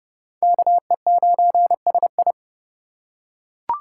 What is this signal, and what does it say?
Morse code, heard as a keyed tone, about 30 wpm, 700 Hz. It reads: KE9SI